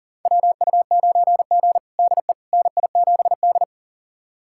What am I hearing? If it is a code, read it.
Morse code, heard as a keyed tone, about 37 wpm, 700 Hz. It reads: WU9G DE NI7D
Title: USA Callsign DE Callsign - 4 Characters - 40wpm